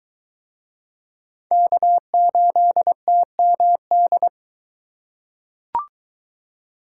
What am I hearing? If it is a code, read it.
Morse code, heard as a keyed tone, about 22 wpm, 700 Hz. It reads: K8TMD